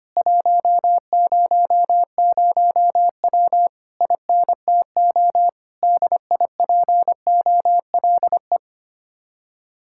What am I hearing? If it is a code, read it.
Morse code, heard as a keyed tone, about 25 wpm, 700 Hz. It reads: 100W INTO DIPOLE
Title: Speed-Racing Common QSO Phrases - 25wpm